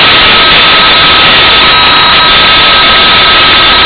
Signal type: PSK31